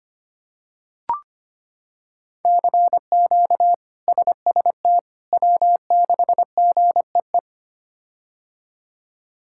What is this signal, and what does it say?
Morse code, heard as a keyed tone, about 25 wpm, 700 Hz. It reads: CQ SST W6GEE
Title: K1USN SST QSO Practice - 25wpm